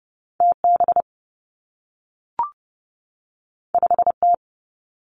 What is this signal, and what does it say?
Morse code, heard as a keyed tone, about 30 wpm, 700 Hz. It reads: T B 5 T
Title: Two Challenging Letter-Number Combinations 30wpm